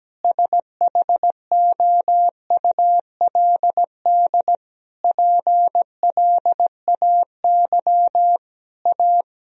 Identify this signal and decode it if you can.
Morse code, heard as a keyed tone, about 17 wpm, 700 Hz. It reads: SHOULD PLAY A
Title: Sentences from Top 1000 Words - Review Part 3 - 17wpm